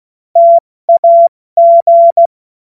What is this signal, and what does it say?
Morse code, heard as a keyed tone, about 16 wpm, 680 Hz. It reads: TAG
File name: Morsekurs / Witze / Witz 22. Februar 2022 70 F80